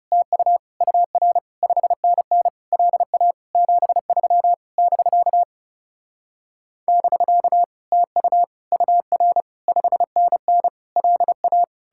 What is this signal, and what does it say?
Morse code, heard as a keyed tone, about 40 wpm, 700 Hz. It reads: TU UR 5NN LA 73 <BK> <BK> TU UR 5NN LA